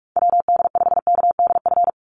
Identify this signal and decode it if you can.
Morse code, heard as a keyed tone, about 44 wpm, 700 Hz. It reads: WD5KDF
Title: USA Call Signs 45wpm